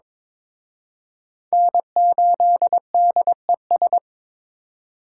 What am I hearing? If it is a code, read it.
Morse code, heard as a keyed tone, about 26 wpm, 700 Hz. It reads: N8DES
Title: Speed Racing - USA 1x3 Call Signs 22wpm